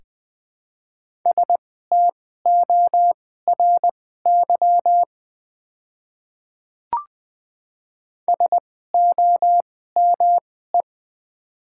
Morse code, heard as a keyed tone, about 20 wpm, spaced as for 14 wpm, 700 Hz. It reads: STORY SOME
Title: 2x Character Spacing - Top 300 Words - 20wpm